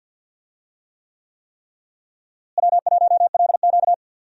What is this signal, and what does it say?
Morse code, heard as a keyed tone, about 52 wpm, 700 Hz. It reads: W1LQ
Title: USA Call Signs - 4 Characters - Repeated 3x - 50wpm